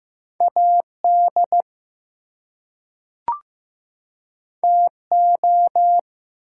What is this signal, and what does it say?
Morse code, heard as a keyed tone, about 15 wpm, 700 Hz. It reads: AD TO
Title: Two Letter Words - 15wpm